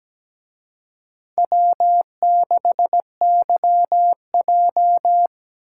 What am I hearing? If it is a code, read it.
Morse code, heard as a keyed tone, about 20 wpm, 700 Hz. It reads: W6YJ
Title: Speed Racing - USA 1x2 Call Signs 17wpm